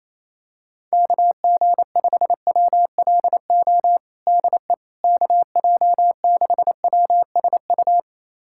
Morse code, heard as a keyed tone, about 28 wpm, 700 Hz. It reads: KG5WLO DE KJ6WSU
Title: USA Callsign DE Callsign 28wpm